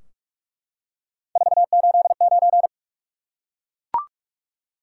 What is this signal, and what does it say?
Morse code, heard as a keyed tone, about 44 wpm, 700 Hz. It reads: 489
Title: Top 180 QSO Elements 45wpm